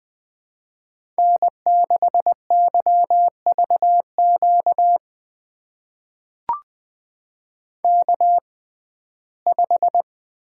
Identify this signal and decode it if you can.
Morse code, heard as a keyed tone, about 21 wpm, spaced as for 15 wpm, 700 Hz. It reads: N6YVQ K5
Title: USA Call Signs - 5 Characters - 6x Character Spacing - 20wpm